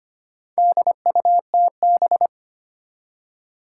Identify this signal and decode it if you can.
Morse code, heard as a keyed tone, about 25 wpm, 700 Hz. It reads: DUTB
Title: Four Challenging Letter-Number Combinations 25wpm